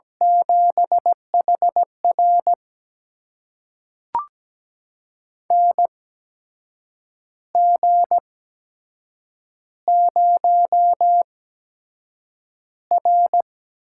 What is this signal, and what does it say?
Morse code, heard as a keyed tone, about 17 wpm, spaced as for 7 wpm, 700 Hz. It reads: 7HR NG0R
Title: USA Call Signs - 8x Character Spacing - 17wpm